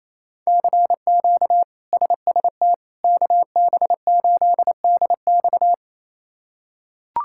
Morse code, heard as a keyed tone, about 28 wpm, 700 Hz. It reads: CQ SST KB8DX
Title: K1USN SST QSO Practice - 28wpm